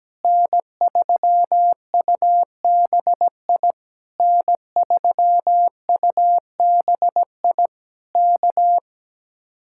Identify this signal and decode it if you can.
Morse code, heard as a keyed tone, about 17 wpm, 700 Hz. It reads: N3UBI N3UBI K
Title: Speed-Racing Common QSO Phrases - 17wpm